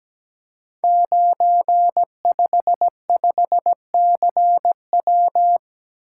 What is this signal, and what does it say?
Morse code, heard as a keyed tone, about 17 wpm, 690 Hz. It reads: 955CW
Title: Code Groups - 5 Letter:Numbers - 17wpm